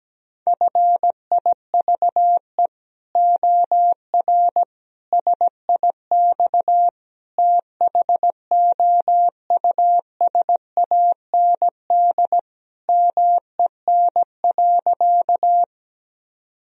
Morse code, heard as a keyed tone, about 17 wpm, 700 Hz. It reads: FIVE OR SIX THOUSAND MEN.
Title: Sentences from Top 1000 Words - Review Part 3 - 17wpm